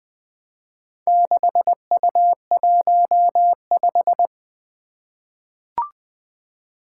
Morse code, heard as a keyed tone, about 20 wpm, 700 Hz. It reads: 6U15
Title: Four Challenging Letter-Number Combinations 20wpm